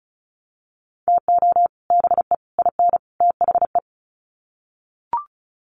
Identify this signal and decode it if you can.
Morse code, heard as a keyed tone, about 34 wpm, 700 Hz. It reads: TO BE IN THE
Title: Sets of 4 Words 35wpm